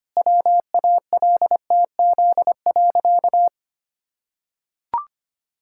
Morse code, heard as a keyed tone, about 25 wpm, 700 Hz. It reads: WALTZ.
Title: Pangram Challenge v2 - 25wpm